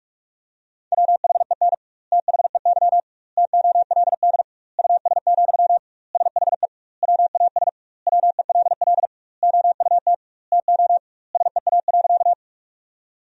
Morse code, heard as a keyed tone, about 46 wpm, 700 Hz. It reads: WHEN THEY TOLD US, SHE WAS WELL OUT TO SEA.
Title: 2x Word-Spacing - Sentences from Top 400 Words - 45wpm